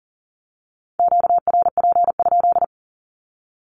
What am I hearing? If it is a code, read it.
Morse code, heard as a keyed tone, about 39 wpm, 700 Hz. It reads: QRP?
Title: Q-Codes with Spoken Meaning - 40wpm